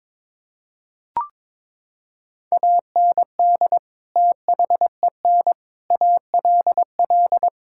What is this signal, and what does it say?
Morse code, heard as a keyed tone, about 21 wpm, 700 Hz. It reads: AND THEN ALL
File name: practice / 22wpm / Sets of 6 Words - 22wpm